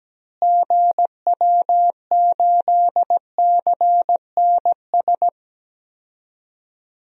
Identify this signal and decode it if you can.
Morse code, heard as a keyed tone, about 17 wpm, 700 Hz. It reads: GW8CNS
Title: DXCC-100 Call Signs - 6 Characters - Repeated 2x - 17wpm